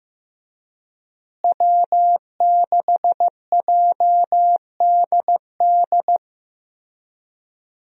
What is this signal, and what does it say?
Morse code, heard as a keyed tone, about 15 wpm, 700 Hz. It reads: W6JDD
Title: USA Call Signs - 5 Characters - Repeated 2x - 15wpm